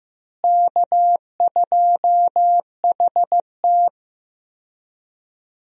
Morse code, heard as a keyed tone, about 16 wpm, 700 Hz. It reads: K2HT
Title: USA Call Signs - 4 Characters - Repeated 3x - 15wpm